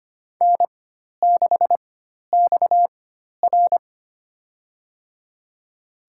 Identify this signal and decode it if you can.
Morse code, heard as a keyed tone, about 25 wpm, spaced as for 12 wpm, 700 Hz. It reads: N6XR
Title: USA Call Signs - 4 Characters - 4x Character Spacing - 25wpm